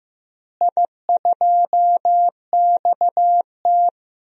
Morse code, heard as a keyed tone, about 15 wpm, 700 Hz. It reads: I2XT
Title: DXCC-100 Call Signs - 4 Characters - Repeated 2x - 15wpm